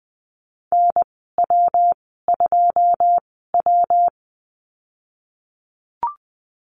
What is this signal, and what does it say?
Morse code, heard as a keyed tone, about 20 wpm, spaced as for 15 wpm, 700 Hz. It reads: NW2W
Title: 122 Lesson 20g - Number 3 - Call Signs 20wpm